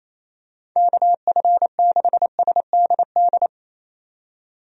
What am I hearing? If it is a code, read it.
Morse code, heard as a keyed tone, about 28 wpm, 700 Hz. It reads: KF6SDD
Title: USA Call Signs - 6 Characters - 28wpm